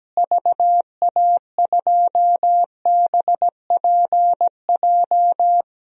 Morse code, heard as a keyed tone, about 17 wpm, 700 Hz. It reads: VA2BPJ
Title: DXCC-100 Call Signs - 6 Characters - Repeated 3x - 17wpm